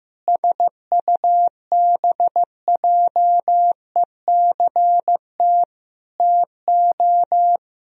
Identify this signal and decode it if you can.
Morse code, heard as a keyed tone, about 15 wpm, 700 Hz. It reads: SUBJECT TO
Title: Sentences from Top 1000 Words - Review Part 1 - 15wpm